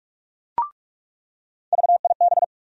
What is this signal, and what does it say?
Morse code, heard as a keyed tone, about 42 wpm, 700 Hz. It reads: VIB
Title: USA Call Sign Postfixes - 3 Letters 45wpm